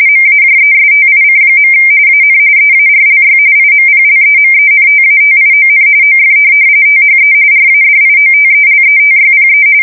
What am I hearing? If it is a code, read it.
RTTY, about 45.45 baud. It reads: MAJESTIC. AN EARLY IMPLEMENTATION OF THE RADIOTELETYPE